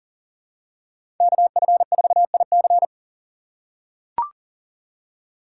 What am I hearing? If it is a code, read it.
Morse code, heard as a keyed tone, about 40 wpm, 700 Hz. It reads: KF4IC
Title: USA Call Signs - 6x Character Spacing - 40wpm